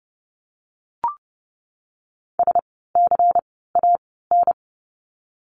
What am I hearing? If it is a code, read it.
Morse code, heard as a keyed tone, about 32 wpm, 700 Hz. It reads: S C A N
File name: Lesson 16f - Letter c - Words 30wpm